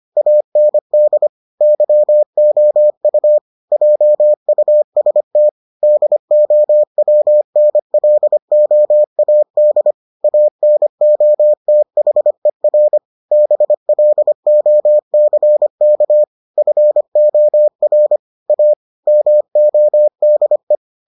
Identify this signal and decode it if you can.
Morse code, heard as a keyed tone, about 25 wpm, 600 Hz. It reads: AND YOU JUST DOWNLOAD ANOTHER BLOCK FOR A MODE